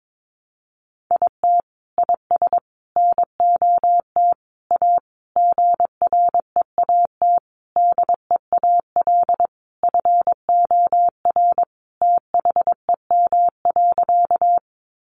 Morse code, heard as a keyed tone, about 22 wpm, 700 Hz. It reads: IT IS NOT A GREAT DEAL FOR THEM.
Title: Sentences from Top 800 Words 22wpm